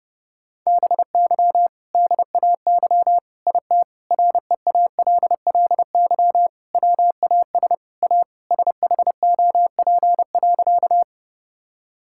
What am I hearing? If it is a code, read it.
Morse code, heard as a keyed tone, about 30 wpm, 700 Hz. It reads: BY DAY IT REALLY WAS A SHOP.
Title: Sentences from Top 1600 Words 30wpm